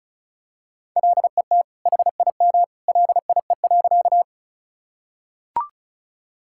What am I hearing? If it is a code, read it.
Morse code, heard as a keyed tone, about 34 wpm, 700 Hz. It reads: LET HIM LIE.
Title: Sentences from Top 1600 Words 35wpm